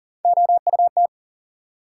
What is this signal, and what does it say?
Morse code, heard as a keyed tone, about 40 wpm, 700 Hz. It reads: OUT
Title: Sentences from Top 100 Words - New Word Drill - 40wpm